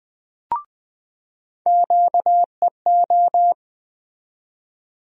Morse code, heard as a keyed tone, about 20 wpm, 700 Hz. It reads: QEO
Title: USA Call Sign Postfixes - 3 Letters 20wpm